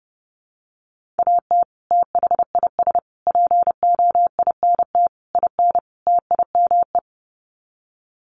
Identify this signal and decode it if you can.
Morse code, heard as a keyed tone, about 30 wpm, 700 Hz. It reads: AT THIS POINT IN TIME
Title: Sets of 5 Words Encore 30wpm